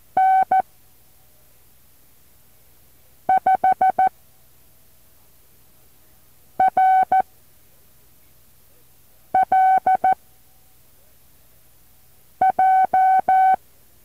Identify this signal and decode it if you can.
Morse code, heard as a keyed tone, about 14 wpm, spaced as for 3 wpm, 760 Hz. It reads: N5RLJ
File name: intl morse code2